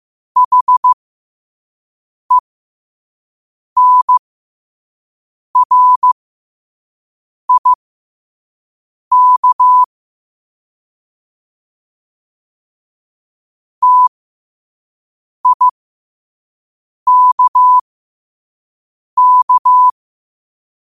Morse code, heard as a keyed tone, about 15 wpm, spaced as for 5 wpm, 1000 Hz. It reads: HENRIK TIKK
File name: qotdfiles / 5wpm